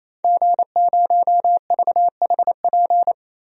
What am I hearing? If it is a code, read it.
Morse code, heard as a keyed tone, about 28 wpm, 700 Hz. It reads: G0VHP
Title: DXCC-100 Call Signs - 5 Characters - 28wpm